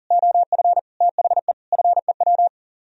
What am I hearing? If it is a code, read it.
Morse code, heard as a keyed tone, about 50 wpm, 700 Hz. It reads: OF THE FEW